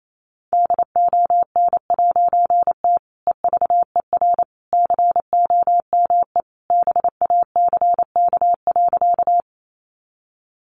Morse code, heard as a keyed tone, about 28 wpm, 700 Hz. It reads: DON'T EVER COME BACK.
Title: A Sound Of Thunder 28wpm